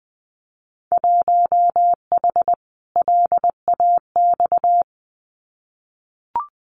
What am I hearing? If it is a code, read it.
Morse code, heard as a keyed tone, about 20 wpm, 700 Hz. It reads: 1H LAX
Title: Winter Field Day Exchanges 20wpm